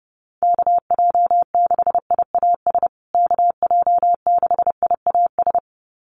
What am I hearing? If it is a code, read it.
Morse code, heard as a keyed tone, about 30 wpm, 700 Hz. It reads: KJ6IAS KJ6IAS
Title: USA Call Signs Repeated 2x 30wpm